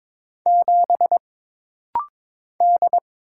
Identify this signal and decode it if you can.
Morse code, heard as a keyed tone, about 22 wpm, 700 Hz. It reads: 7 D
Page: This rapid-fire timing is geared to push your ICR (Instant Character Recognition) skills.